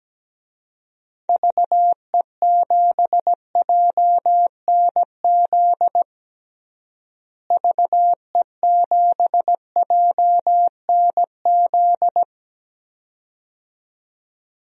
Morse code, heard as a keyed tone, about 17 wpm, 700 Hz. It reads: VE7JNZ VE7JNZ
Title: DXCC-100 Call Signs - 6 Characters - Repeated 2x - 17wpm